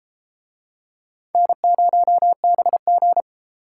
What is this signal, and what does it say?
Morse code, heard as a keyed tone, about 33 wpm, 700 Hz. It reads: N0BG